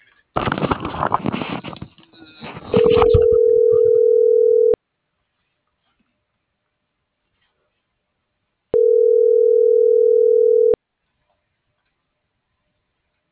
Ringback tone (North America), 440 + 480 Hz.